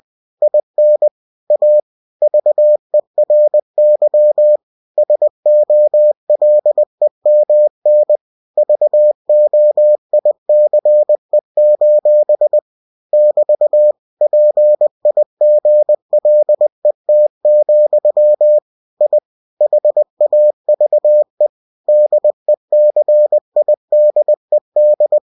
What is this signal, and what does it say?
Morse code, heard as a keyed tone, about 20 wpm, 600 Hz. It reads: IN A VERY SOLEMN VOICE: <BT> PIGLET, I HAVE DECIDED
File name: cw-books / winnie-the-pooh / winnie-20wpm-page-0034